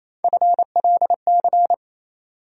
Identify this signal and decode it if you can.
Morse code, heard as a keyed tone, about 28 wpm, 700 Hz. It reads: FLC
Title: USA Call Sign Postfixes - 3 Letters 28wpm